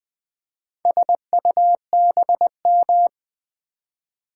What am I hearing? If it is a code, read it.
Morse code, heard as a keyed tone, about 20 wpm, 700 Hz. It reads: SUBM
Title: Build A Word - 2 Letters At A Time - 6-Letter Words - 20wpm